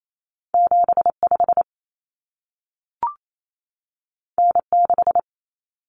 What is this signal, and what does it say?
Morse code, heard as a keyed tone, about 28 wpm, 700 Hz. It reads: 75 N6
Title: Common Cut Numbers - 2-Digit 28wpm